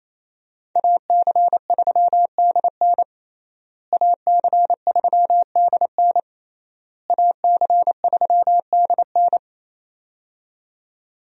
Morse code, heard as a keyed tone, about 27 wpm, 700 Hz. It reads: AC3DN AC3DN AC3DN
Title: USA Call Signs - 5 Characters - Repeated 3x - 28wpm